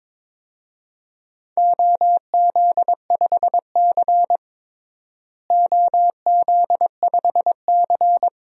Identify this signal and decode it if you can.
Morse code, heard as a keyed tone, about 22 wpm, 700 Hz. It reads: OZ5C OZ5C
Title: DXCC-100 Call Signs - 4 Characters - Repeated 2x - 22wpm